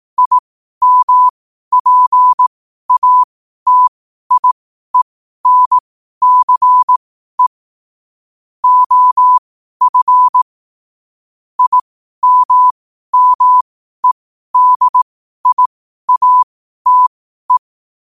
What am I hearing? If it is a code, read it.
Morse code, heard as a keyed tone, about 18 wpm, spaced as for 12 wpm, 1000 Hz. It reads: IMPATIENCE OF IMMEDIATE
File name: qotdfiles / 12wpm